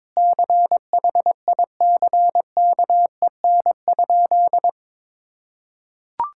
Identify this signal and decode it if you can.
Morse code, heard as a keyed tone, about 22 wpm, 700 Hz. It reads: CHICKEN?
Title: Sentences from Top 1000 Words - Review Part 5 - 22wpm